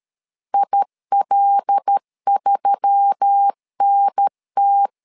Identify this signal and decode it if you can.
Morse code, heard as a keyed tone, about 13 wpm, 800 Hz. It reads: IL3NT
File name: LEZIONI / 5 LBVC / 5 L 12wpm 1913